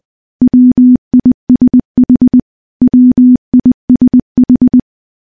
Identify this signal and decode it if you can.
Morse code, heard as a keyed tone, about 20 wpm, 250 Hz. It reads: WISH WISH
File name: morse / morse-training-20WPM